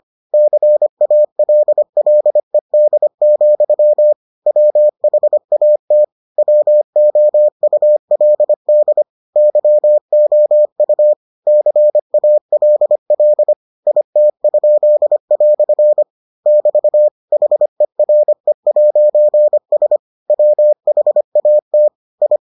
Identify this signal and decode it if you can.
Morse code, heard as a keyed tone, about 25 wpm, 600 Hz. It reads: CALLED, WHAT WOULD YOU CALL IT?" <BT> HERE'S WHAT I